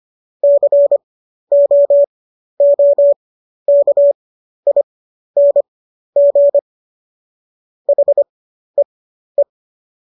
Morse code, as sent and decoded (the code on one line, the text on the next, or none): -.-. --- --- -.- .. -. --. / .... .
COOKING HEE